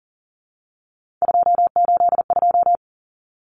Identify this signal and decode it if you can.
Morse code, heard as a keyed tone, about 40 wpm, 700 Hz. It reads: 282
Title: Numbers - 3-Digit 40wpm